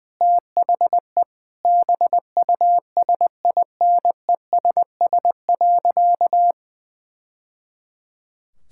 Morse code, heard as a keyed tone, about 20 wpm, 700 Hz. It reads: THE BUSINESS.
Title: Sentences from Top 1000 Words - Review Part 6 - 20wpm